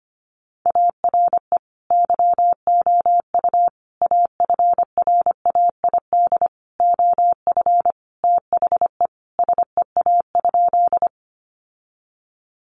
Morse code, heard as a keyed tone, about 25 wpm, 700 Hz. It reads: ARE YOU AFRAID OF THE SEA?
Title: Sentences from Top 1500 Words 25wpm